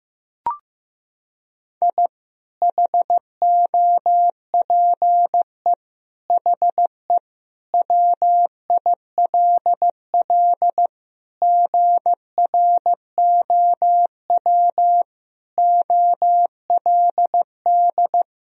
Morse code, heard as a keyed tone, about 16 wpm, 700 Hz. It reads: I HOPE HE WILL GROW OLD
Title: Sentences from Top 1000 Words - Review Part 1 - 15wpm